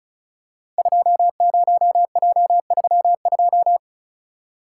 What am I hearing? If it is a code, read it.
Morse code, heard as a keyed tone, about 36 wpm, 690 Hz. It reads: 20J32
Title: Code Groups - 5 Letter:Numbers - 35wpm